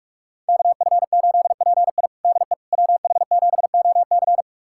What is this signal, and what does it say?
Morse code, heard as a keyed tone, about 46 wpm, 700 Hz. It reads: KF8PI DE WH7OC
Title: USA Callsign DE Callsign - 5 Characters - 45wpm